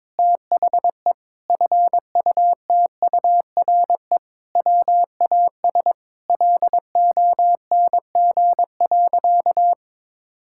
Morse code, heard as a keyed tone, about 22 wpm, 700 Hz. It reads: THE FUTURE WAS LONG.
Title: Sentences from Top 1000 Words - Review Part 6 - 22wpm